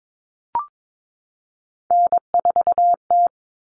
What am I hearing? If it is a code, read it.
Morse code, heard as a keyed tone, about 22 wpm, 700 Hz. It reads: N4T
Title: All Cut Numbers - 3-Digit 22wpm